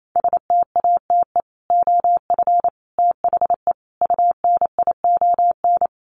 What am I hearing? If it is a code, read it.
Morse code, heard as a keyed tone, about 28 wpm, 700 Hz. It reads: STATE OF THE UNION
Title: Sets of 5 Words Encore 28wpm